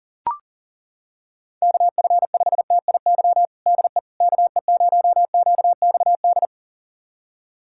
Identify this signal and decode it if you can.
Morse code, heard as a keyed tone, about 40 wpm, 700 Hz. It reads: KF5TIY DE KE0QXD
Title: USA Callsign DE Callsign - 6 Characters - 40wpm